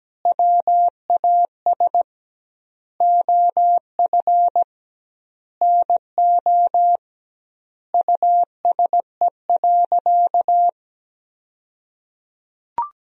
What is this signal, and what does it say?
Morse code, heard as a keyed tone, about 17 wpm, 700 Hz. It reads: WAS OF NO USE.
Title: 2x Word-Spacing - Sentences from Top 100 Words - 17wpm